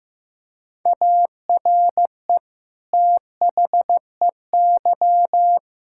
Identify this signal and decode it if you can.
Morse code, heard as a keyed tone, about 15 wpm, 700 Hz. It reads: ARE THEY
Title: Sentences from Top 1000 Words - Review Part 5 - 15wpm